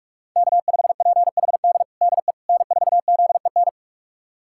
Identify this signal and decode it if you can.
Morse code, heard as a keyed tone, about 47 wpm, 700 Hz. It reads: K5PHD DE N4ZEN
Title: USA Callsign DE Callsign - 5 Characters - 45wpm